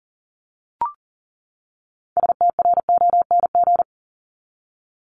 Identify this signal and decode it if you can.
Morse code, heard as a keyed tone, about 40 wpm, 700 Hz. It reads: STRONG
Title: Top 500 Words 40wpm